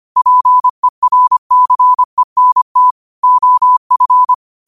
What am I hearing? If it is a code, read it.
Morse code, heard as a keyed tone, about 25 wpm, 1000 Hz. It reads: PERCENT OF